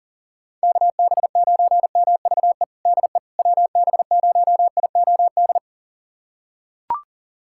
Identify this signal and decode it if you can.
Morse code, heard as a keyed tone, about 40 wpm, 700 Hz. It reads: KB9MVE DE WB0IOD
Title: USA Callsign DE Callsign 40wpm